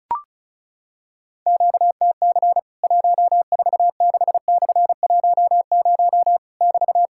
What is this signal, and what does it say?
Morse code, heard as a keyed tone, about 36 wpm, 700 Hz. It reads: QTC 146/10 <BT>
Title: Worked All Europe DX Contest - QTC Exchange - 35wpm